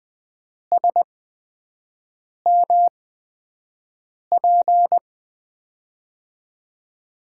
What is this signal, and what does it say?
Morse code, heard as a keyed tone, about 20 wpm, spaced as for 5 wpm, 700 Hz. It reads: SMP
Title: USA Call Signs - 8x Character Spacing - 20wpm